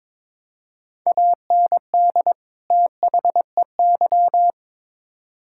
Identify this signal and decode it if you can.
Morse code, heard as a keyed tone, about 21 wpm, 700 Hz. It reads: AND THEY
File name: Build A Phrase - 2-Word Phrases - 22wpm